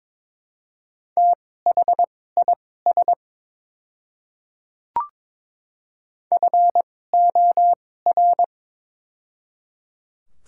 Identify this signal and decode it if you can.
Morse code, heard as a keyed tone, about 21 wpm, spaced as for 16 wpm, 700 Hz. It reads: THIS FOR
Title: 2x Character Spacing - Top 200 Words - 22wpm